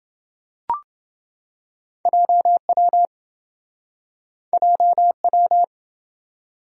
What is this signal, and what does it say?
Morse code, heard as a keyed tone, about 25 wpm, 700 Hz. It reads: JW JW